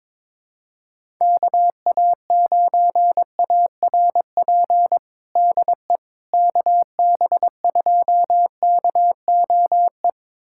Morse code, heard as a keyed tone, about 21 wpm, 700 Hz. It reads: KA9ARP DE KB2KOE
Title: USA Callsign DE Callsign - 6 Characters - 22wpm